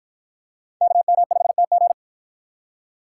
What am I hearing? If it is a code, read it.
Morse code, heard as a keyed tone, about 49 wpm, 700 Hz. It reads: KM5TG